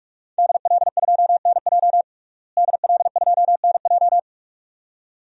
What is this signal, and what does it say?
Morse code, heard as a keyed tone, about 46 wpm, 700 Hz. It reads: DL2NJ DL2NJ
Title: DXCC-100 Call Signs - 5 Characters - Repeated 2x - 45wpm